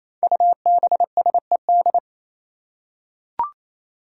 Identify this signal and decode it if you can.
Morse code, heard as a keyed tone, about 28 wpm, 700 Hz. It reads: UBSED
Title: Five Challenging Letter-Number Combinations 28wpm